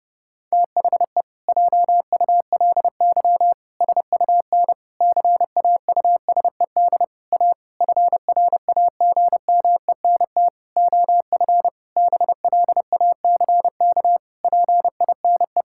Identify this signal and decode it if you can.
Morse code, heard as a keyed tone, about 30 wpm, 700 Hz. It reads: THE JULY SUN CAUSED A FRAGMENT OF BLACK PINE
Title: Pangram Challenge v2 - 30wpm